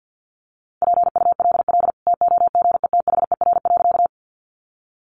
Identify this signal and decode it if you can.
Morse code, heard as a keyed tone, about 50 wpm, 700 Hz. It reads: PULL TOGETHER.
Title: Sentences from Top 400 Words 50wpm